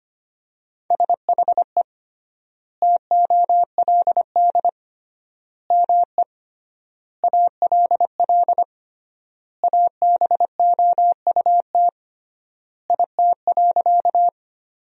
Morse code, heard as a keyed tone, about 25 wpm, 700 Hz. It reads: SHE TOLD ME ALL ABOUT IT.
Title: 3x Word-Spacing - Sentences from Top 400 Words - 25wpm